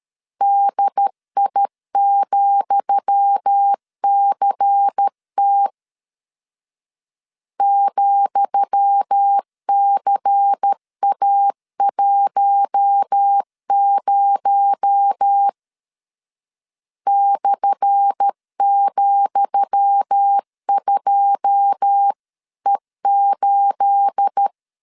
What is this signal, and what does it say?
Morse code, heard as a keyed tone, about 13 wpm, 800 Hz. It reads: DI,CT ,CA10 /,2E8
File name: LEZIONI / 10 SIGNS / 10D COMA 12wpm 2544